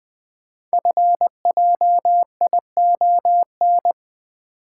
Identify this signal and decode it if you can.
Morse code, heard as a keyed tone, about 20 wpm, 700 Hz. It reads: FJION
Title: Code Groups - 5 Letters - 20wpm